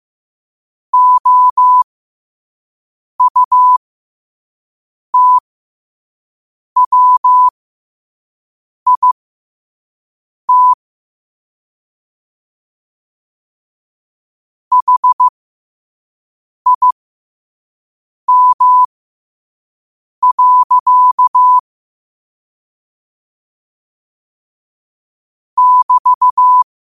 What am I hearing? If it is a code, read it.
Morse code, heard as a keyed tone, about 15 wpm, spaced as for 5 wpm, 1000 Hz. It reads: OUTWIT HIM. <BT>